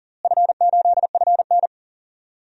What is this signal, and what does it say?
Morse code, heard as a keyed tone, about 39 wpm, 700 Hz. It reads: F8FN
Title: DXCC-100 Call Signs - 4 Characters - Repeated 2x - 40wpm